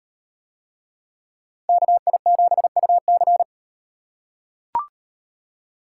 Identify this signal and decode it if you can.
Morse code, heard as a keyed tone, about 38 wpm, 700 Hz. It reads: KI7UC